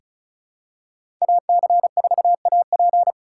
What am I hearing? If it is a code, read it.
Morse code, heard as a keyed tone, about 36 wpm, 700 Hz. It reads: AC4AP
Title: USA Call Signs - 5 Characters - 4x Character Spacing - 35wpm